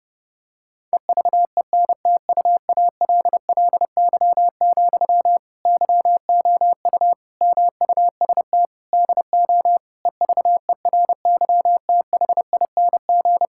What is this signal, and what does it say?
Morse code, heard as a keyed tone, about 32 wpm, 700 Hz. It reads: EVENTUALLY, YOU MUST DO EVERYTHING
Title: Sentences from Top 1000 Words - Review Part 6 - 30wpm